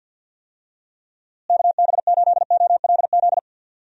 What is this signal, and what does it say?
Morse code, heard as a keyed tone, about 52 wpm, 700 Hz. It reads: KB8OLZ
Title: USA Call Signs - 6 Characters - Repeated 3x - 50wpm